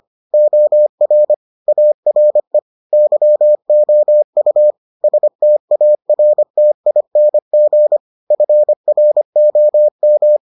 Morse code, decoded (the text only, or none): OR ARE YOU STARTING FROM